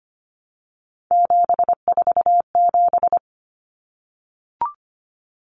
Morse code, heard as a keyed tone, about 25 wpm, 700 Hz. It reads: 747
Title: Common Cut Numbers - 3-Digit 25wpm